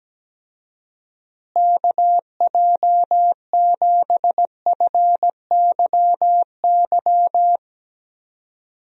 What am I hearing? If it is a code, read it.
Morse code, heard as a keyed tone, about 17 wpm, 700 Hz. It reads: KJ7FYY
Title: USA Call Signs - 6 Characters - Repeated 3x - 17wpm